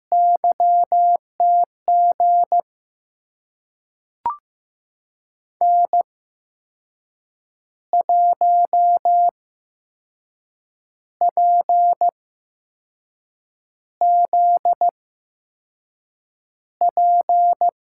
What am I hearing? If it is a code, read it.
Morse code, heard as a keyed tone, about 15 wpm, spaced as for 5 wpm, 700 Hz. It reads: YTG N1PZP
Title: USA Call Signs - 5 Characters - 8x Character Spacing - 15wpm